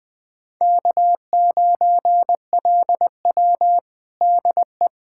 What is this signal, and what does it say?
Morse code, heard as a keyed tone, about 21 wpm, 700 Hz. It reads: K9LW DE
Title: USA Callsign DE Callsign - 4 Characters - 20wpm